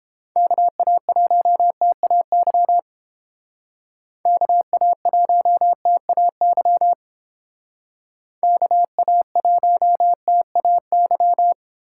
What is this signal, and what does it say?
Morse code, heard as a keyed tone, about 30 wpm, 700 Hz. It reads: KA1TAY KA1TAY KA1TAY